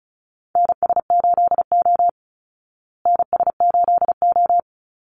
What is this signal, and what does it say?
Morse code, heard as a keyed tone, about 34 wpm, 700 Hz. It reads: NS8O NS8O
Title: Call Signs Repeated 2x - 35wpm